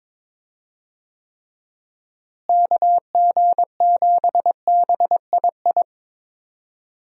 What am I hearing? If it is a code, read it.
Morse code, heard as a keyed tone, about 22 wpm, 700 Hz. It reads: KG7BII
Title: USA Call Signs - 6 Characters - Repeated 2x - 22wpm